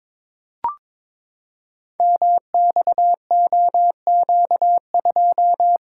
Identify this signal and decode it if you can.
Morse code, heard as a keyed tone, about 22 wpm, 700 Hz. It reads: MXOQ2
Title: Code Groups - 5 Letter:Numbers - 22wpm